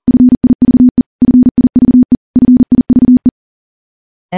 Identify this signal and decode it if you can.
Morse code, heard as a keyed tone, about 40 wpm, 250 Hz. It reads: FIVE FIVE FIVE
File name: morse / morse-training-40WPM